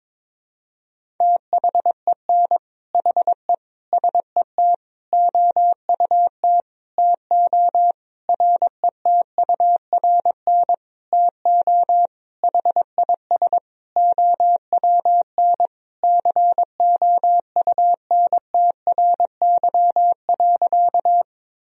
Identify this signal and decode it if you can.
Morse code, heard as a keyed tone, about 22 wpm, 700 Hz. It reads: THEN HE SET OUT TO RETURN TO HIS OWN COUNTRY.
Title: Sentences from Top 1000 Words - Review Part 1 - 22wpm